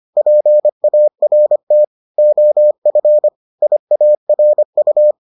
Morse code, heard as a keyed tone, about 25 wpm, 600 Hz. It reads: PART OF IARU